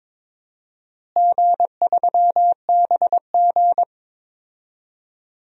Morse code, heard as a keyed tone, about 22 wpm, 700 Hz. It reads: G3BG
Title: DXCC-100 Call Signs - 4 Characters - Repeated 2x - 22wpm